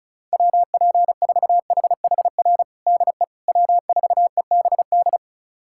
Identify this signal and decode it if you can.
Morse code, heard as a keyed tone, about 34 wpm, 700 Hz. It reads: WP4HHR DE W4EBD
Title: USA Callsign DE Callsign 35wpm